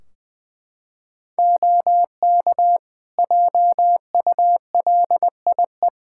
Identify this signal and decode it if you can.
Morse code, heard as a keyed tone, about 20 wpm, 700 Hz. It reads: OK JULIE
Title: Practice QSO - Exchange 3 - More Info - 20wpm